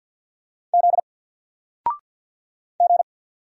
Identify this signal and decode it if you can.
Morse code, heard as a keyed tone, about 48 wpm, 700 Hz. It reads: Z G
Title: Single Character (Letter:Number:Punctuation:Prosign) 50wpm